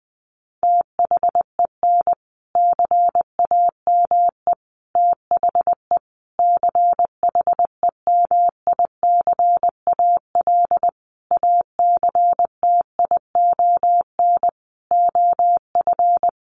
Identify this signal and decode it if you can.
Morse code, heard as a keyed tone, about 20 wpm, 700 Hz. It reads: THEN CAME THE CHEMICAL ACTION OF
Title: Sentences from Top 1200 Words 20wpm